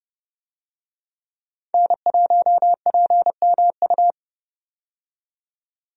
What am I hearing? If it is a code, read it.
Morse code, heard as a keyed tone, about 32 wpm, 700 Hz. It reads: N1PMU
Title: USA Call Signs - 5 Characters - 30wpm